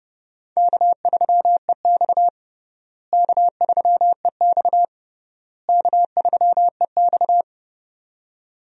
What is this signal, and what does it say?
Morse code, heard as a keyed tone, about 30 wpm, 700 Hz. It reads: K3EX K3EX K3EX
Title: USA Call Signs - 4 Characters - Repeated 3x - 30wpm